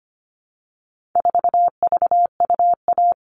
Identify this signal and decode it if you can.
Morse code, heard as a keyed tone, about 25 wpm, 700 Hz. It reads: 4VUA
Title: All Cut Numbers - 4-Digit 25wpm